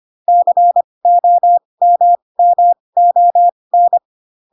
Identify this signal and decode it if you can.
Morse code, heard as a keyed tone, about 25 wpm, spaced as for 20 wpm, 700 Hz. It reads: COMMON
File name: Pinocchio / Pinocchio ch1 15 25 ik1hge